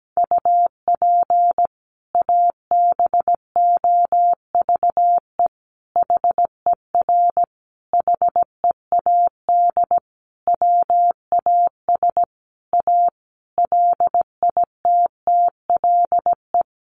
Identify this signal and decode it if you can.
Morse code, heard as a keyed tone, about 17 wpm, 700 Hz. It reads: UP ABOVE HER HEAD WAS A LITTLE
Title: Sentences from Top 1200 Words 17wpm